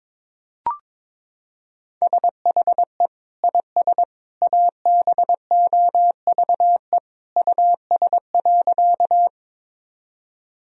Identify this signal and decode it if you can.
Morse code, heard as a keyed tone, about 22 wpm, 700 Hz. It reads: SHE IS ABOVE US.
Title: Sentences from Top 1000 Words - Review Part 2 - 22wpm